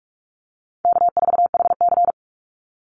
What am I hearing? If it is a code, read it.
Morse code, heard as a keyed tone, about 44 wpm, 700 Hz. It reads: K4HC
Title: Easy Call Signs - 45wpm